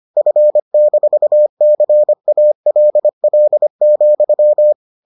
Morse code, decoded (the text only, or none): F-CALL,